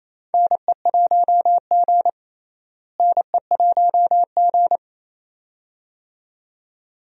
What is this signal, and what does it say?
Morse code, heard as a keyed tone, about 28 wpm, 700 Hz. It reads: NE1G NE1G
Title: USA Call Signs - 4 Characters - Repeated 2x - 28wpm